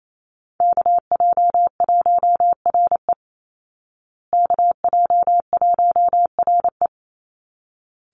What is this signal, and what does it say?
Morse code, heard as a keyed tone, about 28 wpm, 700 Hz. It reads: KJ1RE KJ1RE
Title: Call Signs Repeated 3x - 28wpm